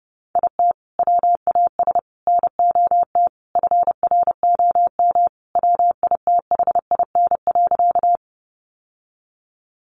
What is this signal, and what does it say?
Morse code, heard as a keyed tone, about 30 wpm, 700 Hz. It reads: IT WAS NOT FROM WITHIN.
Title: Sentences from Top 700 Words 30wpm